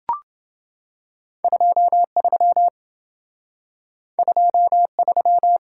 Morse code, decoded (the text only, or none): 23 23